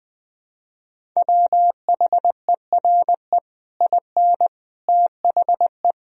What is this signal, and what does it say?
Morse code, heard as a keyed tone, about 20 wpm, 700 Hz. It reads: WHERE IN THE
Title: Sentences from Top 200 Words and Abbreviations 20wpm